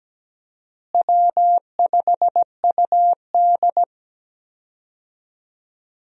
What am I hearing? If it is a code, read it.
Morse code, heard as a keyed tone, about 17 wpm, 700 Hz. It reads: W5UD
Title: USA Call Signs - 4 Characters - Repeated 3x - 17wpm